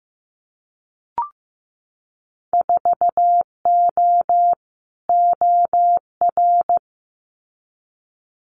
Morse code, heard as a keyed tone, about 15 wpm, 700 Hz. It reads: 4O OR
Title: Winter Field Day Exchanges Encore 15wpm